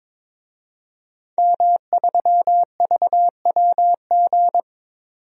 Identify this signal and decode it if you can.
Morse code, heard as a keyed tone, about 22 wpm, 700 Hz. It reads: M3VWG
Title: DXCC-100 Call Signs - 5 Characters - 22wpm